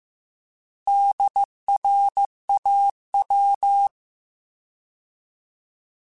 Morse code, heard as a keyed tone, about 15 wpm, 800 Hz. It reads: DRAW
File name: Alphabetical / Alphabetical mp3s / Ds @13